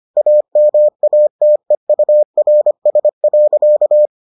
Morse code, heard as a keyed tone, about 25 wpm, 600 Hz. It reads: AMATEURS.